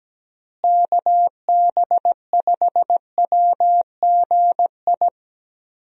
Morse code, heard as a keyed tone, about 17 wpm, 700 Hz. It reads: KB5WGI
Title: USA Call Signs - 6 Characters - Repeated 3x - 17wpm